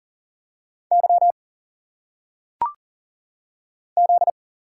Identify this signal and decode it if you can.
Morse code, heard as a keyed tone, about 43 wpm, 700 Hz. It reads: Y Z
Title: Single Letters - Phonetic Pronunciation 40wpm